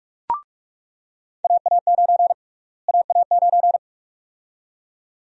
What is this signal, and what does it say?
Morse code, heard as a keyed tone, about 41 wpm, 700 Hz. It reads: AA9 AA9
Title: USA Call Sign Prefixes - Repeated 2x - 45wpm